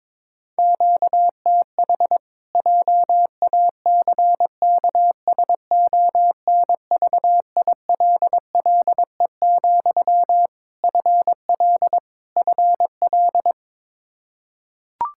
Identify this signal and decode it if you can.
Morse code, heard as a keyed tone, about 22 wpm, 700 Hz. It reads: QTH JACKSONVILLE, FL FL
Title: Practice QSO - Exchange 1 - Beginning - 22wpm